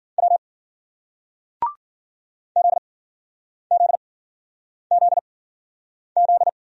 Morse code, heard as a keyed tone, about 50 wpm, 700 Hz. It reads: W Z Z Z Z
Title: Speed-Racing Single Letters 40wpm